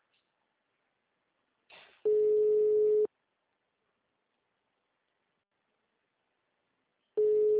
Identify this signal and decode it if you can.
Ringback tone (Europe, 425 Hz).